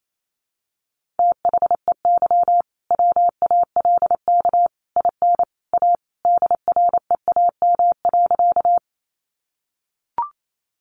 Morse code, heard as a keyed tone, about 28 wpm, 700 Hz. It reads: THEY WALK IN A DREAM.
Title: Sentences from Top 1500 Words 28wpm